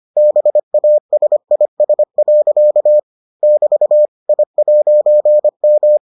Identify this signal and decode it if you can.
Morse code, heard as a keyed tone, about 25 wpm, 600 Hz. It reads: BASIS. <BT> I'M